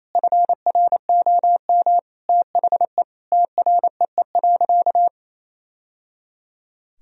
Morse code, heard as a keyed tone, about 28 wpm, 700 Hz. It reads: FROM THE TREE.
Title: Sentences from Top 1000 Words - Review Part 1 - 28wpm